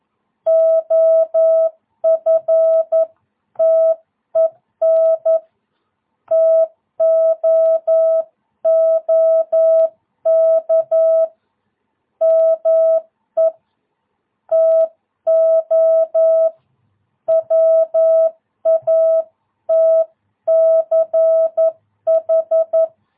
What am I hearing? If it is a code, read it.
Morse code, heard as a keyed tone, about 11 wpm, 650 Hz. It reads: OFTEN TOOK ME TO WATCH
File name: jh8cbh / 2022cw